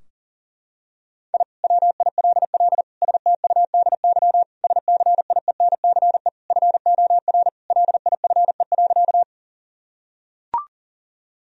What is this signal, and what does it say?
Morse code, heard as a keyed tone, about 40 wpm, 700 Hz. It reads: I WILL STUDY SCIENCE FOR LIFE.
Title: Sentences from Top 1000 Words - Review Part 2 - 40wpm